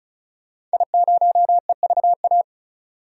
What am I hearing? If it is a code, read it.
Morse code, heard as a keyed tone, about 35 wpm, 700 Hz. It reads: I0EVA
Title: DXCC-100 Call Signs - 5 Characters - 35wpm